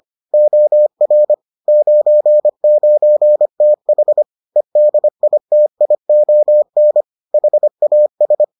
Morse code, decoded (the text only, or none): OR 99TH EDITION HAS